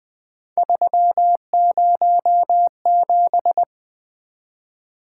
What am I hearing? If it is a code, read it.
Morse code, heard as a keyed tone, about 20 wpm, 700 Hz. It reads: 307
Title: Three Letter-Number Combinations - 20wpm